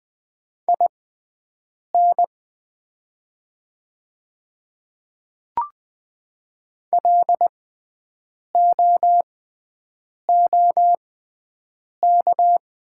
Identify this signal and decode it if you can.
Morse code, heard as a keyed tone, about 20 wpm, spaced as for 7 wpm, 700 Hz. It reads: IN LOOK
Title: 6x Character Spacing - Top 100 Words - 20wpm